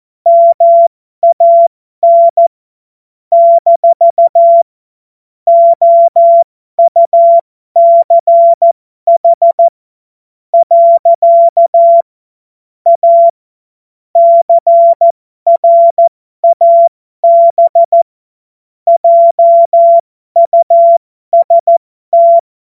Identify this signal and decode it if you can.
Morse code, heard as a keyed tone, about 14 wpm, spaced as for 12 wpm, 680 Hz. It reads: MAN - OUCH . A CRAB JUST
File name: Morsekurs / Witze